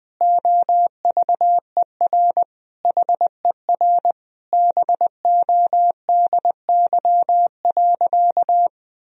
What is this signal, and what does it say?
Morse code, heard as a keyed tone, about 20 wpm, 700 Hz. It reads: OVER HER BODY.
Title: Sentences from Top 1000 Words - Review Part 2 - 20wpm